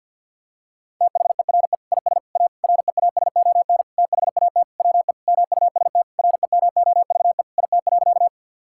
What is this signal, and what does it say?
Morse code, heard as a keyed tone, about 50 wpm, 700 Hz. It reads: THERE IS A REASON THAT WE MUST REMOVE IT.
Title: Sentences from Top 1000 Words - Review Part 5 - 50wpm